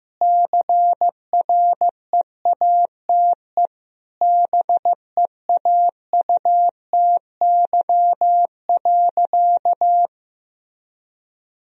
Morse code, heard as a keyed tone, about 16 wpm, 700 Hz. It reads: CREATE BEAUTY.
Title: Sentences from Top 1000 Words - Review Part 3 - 15wpm